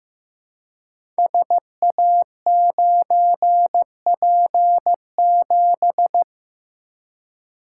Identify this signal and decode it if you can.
Morse code, heard as a keyed tone, about 15 wpm, 700 Hz. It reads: SA9P7
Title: Code Groups - 5 Letter:Numbers - 15wpm